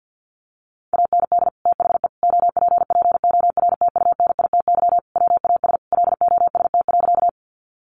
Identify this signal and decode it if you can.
Morse code, heard as a keyed tone, about 50 wpm, 700 Hz. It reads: AND THE OPPORTUNITY WAS LOST.
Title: Sentences from Top 1400 Words 50wpm